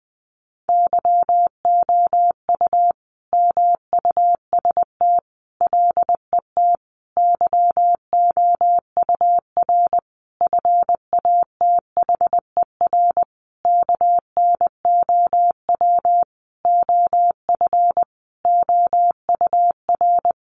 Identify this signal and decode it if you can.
Morse code, heard as a keyed tone, about 20 wpm, 700 Hz. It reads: YOU MUST LET YOUR FATHER KNOW OF OUR
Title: Sentences from Top 1500 Words 20wpm